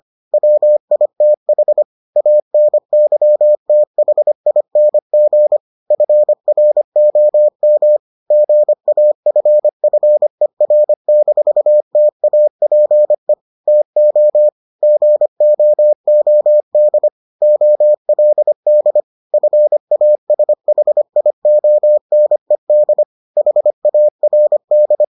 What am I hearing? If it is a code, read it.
Morse code, heard as a keyed tone, about 25 wpm, 600 Hz. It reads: WITH ANYTHING FROM GAFFER-TAPE TO GOOD OLD FASHIONED HARD